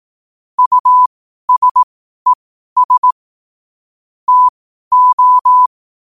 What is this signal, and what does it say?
Morse code, heard as a keyed tone, about 18 wpm, spaced as for 12 wpm, 1000 Hz. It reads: USES TO
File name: qotdfiles / 12wpm